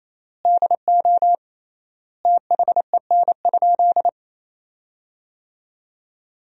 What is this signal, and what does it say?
Morse code, heard as a keyed tone, about 28 wpm, 700 Hz. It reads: DO THEN?
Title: 3x Word-Spacing - Sentences from Top 100 Words - 28wpm